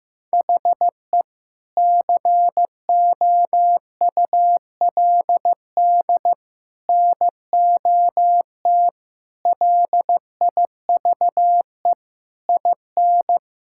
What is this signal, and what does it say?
Morse code, heard as a keyed tone, about 15 wpm, 700 Hz. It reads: HE COULD NOT LIVE IN
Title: Sentences from Top 1600 Words 15wpm